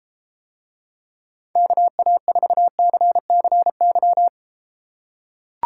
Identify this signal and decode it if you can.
Morse code, heard as a keyed tone, about 33 wpm, 700 Hz. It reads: KA4CCY